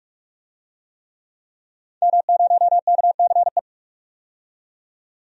Morse code, heard as a keyed tone, about 42 wpm, 700 Hz. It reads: M0KCE
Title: DXCC-100 Call Signs - 5 Characters - 45wpm